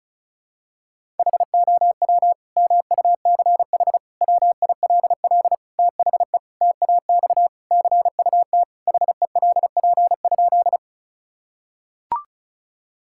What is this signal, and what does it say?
Morse code, heard as a keyed tone, about 36 wpm, 700 Hz. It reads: HOW MUCH WILL THE TAX CUT HELP?
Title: Sentences from Top 1000 Words - Review Part 5 - 35wpm